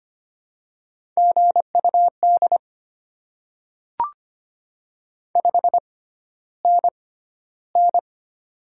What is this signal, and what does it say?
Morse code, heard as a keyed tone, about 25 wpm, spaced as for 13 wpm, 690 Hz. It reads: GUD 5NN
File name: Top 20 QSO Elements - 6x Character Spacing - 25wpm